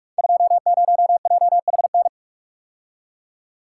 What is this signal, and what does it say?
Morse code, heard as a keyed tone, about 44 wpm, 700 Hz. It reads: 20JHN
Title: Code Groups - 5 Letter:Numbers - 45wpm